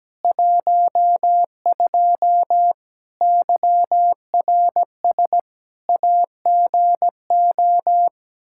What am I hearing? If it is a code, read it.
Morse code, heard as a keyed tone, about 17 wpm, 700 Hz. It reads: 12 YRS AGO
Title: K1USN SST QSO Practice 17wpm